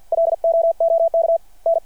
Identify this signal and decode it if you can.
Morse code, heard as a keyed tone, about 50 wpm, 660 Hz. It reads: LOOK N
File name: archived / CRSnet